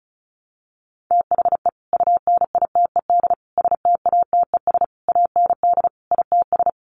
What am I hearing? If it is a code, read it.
Morse code, heard as a keyed tone, about 34 wpm, 700 Hz. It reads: THE UNITED STATES AND ITS
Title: Sets of 5 Words Encore 35wpm